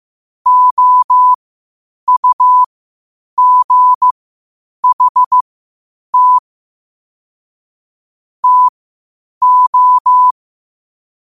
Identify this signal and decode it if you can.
Morse code, heard as a keyed tone, about 15 wpm, spaced as for 8 wpm, 1000 Hz. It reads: OUGHT TO
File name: qotdfiles / 8wpm